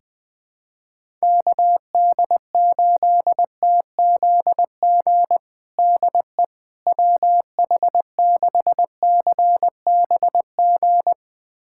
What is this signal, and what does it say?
Morse code, heard as a keyed tone, about 20 wpm, 700 Hz. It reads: KD8TZG DE WH6CBG
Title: USA Callsign DE Callsign - 6 Characters - 20wpm